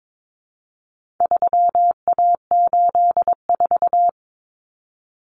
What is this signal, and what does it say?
Morse code, heard as a keyed tone, about 22 wpm, 700 Hz. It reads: 3A84
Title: Common Cut Numbers - 4-Digit 22wpm